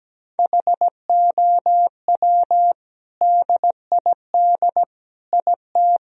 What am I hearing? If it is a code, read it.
Morse code, heard as a keyed tone, about 17 wpm, 700 Hz. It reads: HOW DID IT
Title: Sentences from Top 1000 Words - Review Part 1 - 17wpm